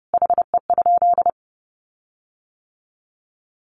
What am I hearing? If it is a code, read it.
Morse code, heard as a keyed tone, about 30 wpm, 700 Hz. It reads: HE?
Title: Sentences from Top 1300 Words 30wpm